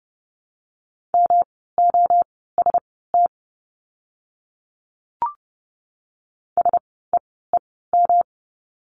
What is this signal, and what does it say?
Morse code, heard as a keyed tone, about 30 wpm, spaced as for 17 wpm, 700 Hz. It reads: MOST SEEM
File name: course / 30:10wpm / 230 Lesson 37f - Letter x - Words 30wpm